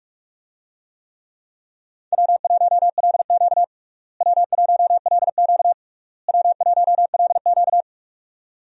Morse code, heard as a keyed tone, about 46 wpm, 700 Hz. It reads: W1LQ W1LQ W1LQ
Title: USA Call Signs - 4 Characters - Repeated 3x - 45wpm